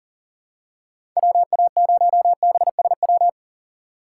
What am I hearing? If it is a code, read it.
Morse code, heard as a keyed tone, about 37 wpm, 700 Hz. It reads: WA0BSW
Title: USA Call Signs - 4x Character Spacing - 40wpm